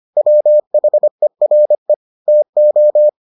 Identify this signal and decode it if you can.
Morse code, heard as a keyed tone, about 25 wpm, 600 Hz. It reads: WHERE TO